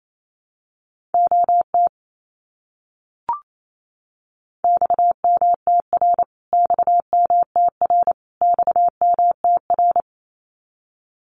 Morse code, heard as a keyed tone, about 28 wpm, 700 Hz. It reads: OT XMTR XMTR XMTR
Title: Abbreviations Repeated 3x - 28wpm